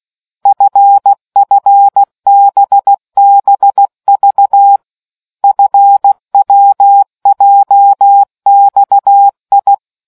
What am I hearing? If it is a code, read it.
Morse code, heard as a keyed tone, about 16 wpm, 800 Hz. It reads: FFBBV FWJXI